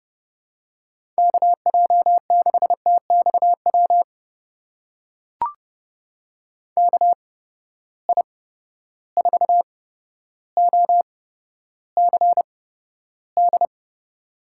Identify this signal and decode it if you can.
Morse code, heard as a keyed tone, about 31 wpm, spaced as for 13 wpm, 700 Hz. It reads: KJ6TXW KI4OCD
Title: USA Call Signs - 6 Characters - 8x Character Spacing - 30wpm